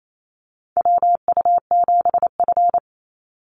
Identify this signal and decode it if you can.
Morse code, heard as a keyed tone, about 28 wpm, 700 Hz. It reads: WU7F
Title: Call Signs - 28wpm